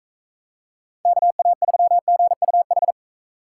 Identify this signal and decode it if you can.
Morse code, heard as a keyed tone, about 40 wpm, 700 Hz. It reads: KA3GUH